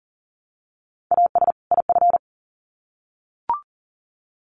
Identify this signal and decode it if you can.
Morse code, heard as a keyed tone, about 40 wpm, 700 Hz. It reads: AS IF
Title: Sets of 2 Words 40wpm